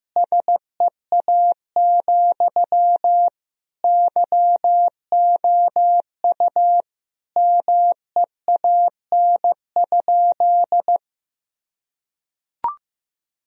Morse code, heard as a keyed tone, about 15 wpm, 700 Hz. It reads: SEA, YOU MEAN?
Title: Sentences from Top 1000 Words - Review Part 1 - 15wpm